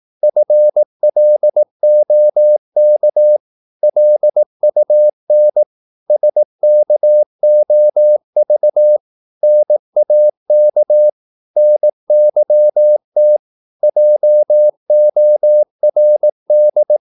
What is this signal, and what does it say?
Morse code, heard as a keyed tone, about 18 wpm, 600 Hz. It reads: FLOK LUN SKOV NAK NYT JORD
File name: Korte ord DK 18wpm